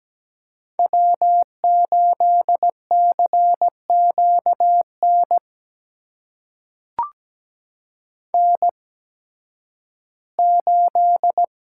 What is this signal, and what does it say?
Morse code, heard as a keyed tone, about 17 wpm, spaced as for 11 wpm, 700 Hz. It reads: W8CQN N8
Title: USA Call Signs - 5 Characters - 8x Character Spacing - 17wpm